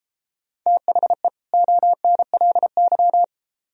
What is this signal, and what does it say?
Morse code, heard as a keyed tone, about 33 wpm, 700 Hz. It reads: THE ONLY